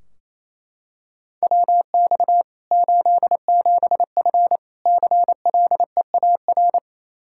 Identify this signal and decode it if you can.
Morse code, heard as a keyed tone, about 27 wpm, 700 Hz. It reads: WX 87F CLEAR
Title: Practice QSO - Exchange 2 - Info - 28wpm